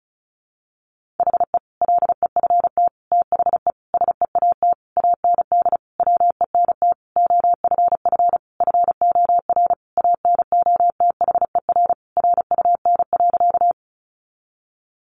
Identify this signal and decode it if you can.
Morse code, heard as a keyed tone, about 34 wpm, 700 Hz. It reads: HE LEFT THE SEAT AND WENT OFF FOR ANOTHER RUN.
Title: Sentences from Top 1500 Words 35wpm